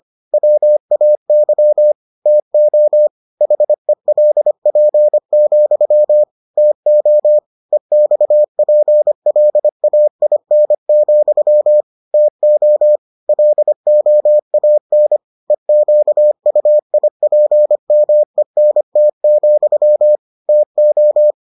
Morse code, heard as a keyed tone, about 25 wpm, 600 Hz. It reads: WAY TO HELP, TO EXPLAIN, TO LOAN EQUIPMENT, TO